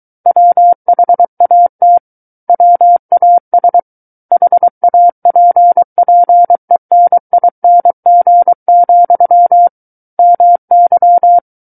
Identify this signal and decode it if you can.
Morse code, heard as a keyed tone, about 23 wpm, 700 Hz. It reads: WHAT WAS HAPPENING, MY
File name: Book One / 14-Chapter 14 - 23 WPM